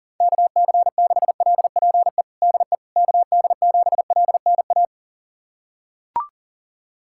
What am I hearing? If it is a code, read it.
Morse code, heard as a keyed tone, about 37 wpm, 700 Hz. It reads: KC6LPE DE KD7LNA
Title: USA Callsign DE Callsign - 6 Characters - 40wpm